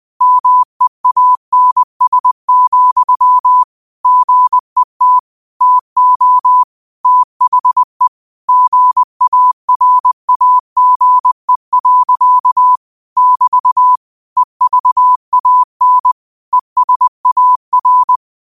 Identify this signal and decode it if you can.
Morse code, heard as a keyed tone, about 20 wpm, 1000 Hz. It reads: MEANS, GET TO THE GARAGE. <BT> EVAN ESAR
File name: qotdfiles / 20wpm